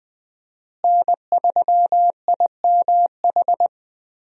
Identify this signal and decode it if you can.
Morse code, heard as a keyed tone, about 20 wpm, 700 Hz. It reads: N3IMH
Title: USA Call Signs - 5 Characters - Repeated 2x - 20wpm